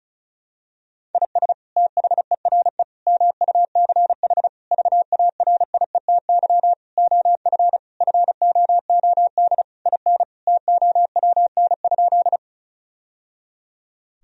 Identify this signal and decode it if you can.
Morse code, heard as a keyed tone, about 34 wpm, 700 Hz. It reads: IS THERE MUCH VARIETY OF FOOD IN TOWN?
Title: Sentences from Top 1000 Words - Review Part 4 - 35wpm